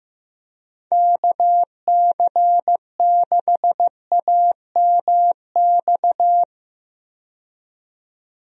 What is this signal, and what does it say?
Morse code, heard as a keyed tone, about 15 wpm, 700 Hz. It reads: KC6AMX
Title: USA Call Signs - 6 Characters - 15wpm